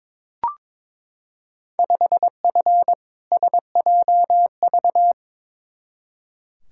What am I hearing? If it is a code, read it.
Morse code, heard as a keyed tone, about 22 wpm, 700 Hz. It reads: 5F SJV
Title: Field Day Exchanges Encore - 22wpm